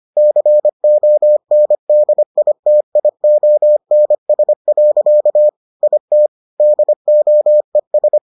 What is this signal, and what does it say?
Morse code, heard as a keyed tone, about 25 wpm, 600 Hz. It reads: CONDITIONS. IT DOES